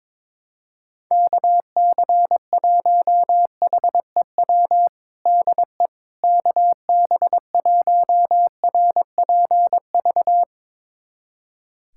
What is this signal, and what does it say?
Morse code, heard as a keyed tone, about 22 wpm, 700 Hz. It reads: KC1HEW DE KB1RPV
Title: USA Callsign DE Callsign - 6 Characters - 22wpm